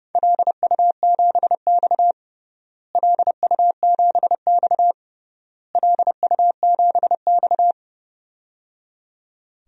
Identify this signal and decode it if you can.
Morse code, heard as a keyed tone, about 30 wpm, 700 Hz. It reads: LU7X LU7X LU7X
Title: DXCC-100 Call Signs - 4 Characters - Repeated 3x - 30wpm